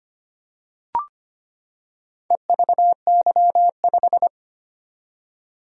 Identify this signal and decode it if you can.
Morse code, heard as a keyed tone, about 25 wpm, 700 Hz. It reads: EVY5
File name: practice / 25wpm / Four Letter-Number Combinations - 25wpm